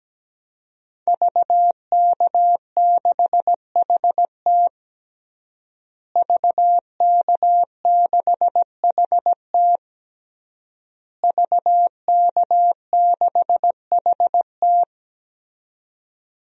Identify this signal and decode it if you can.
Morse code, heard as a keyed tone, about 17 wpm, 700 Hz. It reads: VK6HT VK6HT VK6HT
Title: DXCC-100 Call Signs - 5 Characters - Repeated 3x - 17wpm